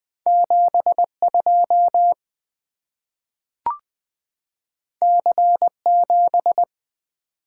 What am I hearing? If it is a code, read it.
Morse code, heard as a keyed tone, about 20 wpm, 700 Hz. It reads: 7 2 C 7
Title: Two Letter-Number Combinations - 20wpm